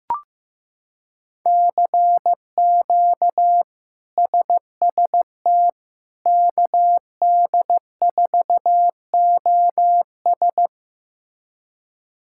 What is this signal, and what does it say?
Morse code, heard as a keyed tone, about 15 wpm, 700 Hz. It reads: CQ SST KD4OS
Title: K1USN SST QSO Practice - 15wpm